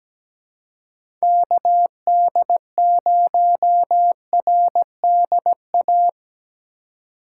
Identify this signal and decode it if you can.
Morse code, heard as a keyed tone, about 17 wpm, 700 Hz. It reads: KD0RDA
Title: USA Call Signs - 4x Character Spacing - 17wpm